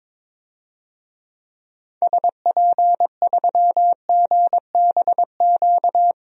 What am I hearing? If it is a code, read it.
Morse code, heard as a keyed tone, about 22 wpm, 700 Hz. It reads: SP3GBQ
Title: DXCC-100 Call Signs - 6 Characters - 22wpm